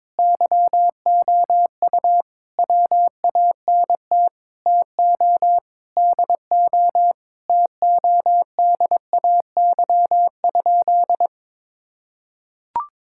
Morse code, heard as a keyed tone, about 22 wpm, 700 Hz. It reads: YOU WANT TO DO TODAY?
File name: practice / 22wpm / Sentences with 4-Word Phrases - 22wpm